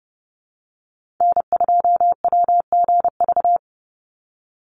Morse code, heard as a keyed tone, about 30 wpm, 700 Hz. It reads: N2WGV
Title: USA Call Signs 30wpm